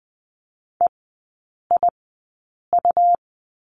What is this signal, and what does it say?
Morse code, heard as a keyed tone, about 20 wpm, spaced as for 8 wpm, 700 Hz. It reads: EIU
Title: 100 Lesson 17e - Letter u - Three Characters 20wpm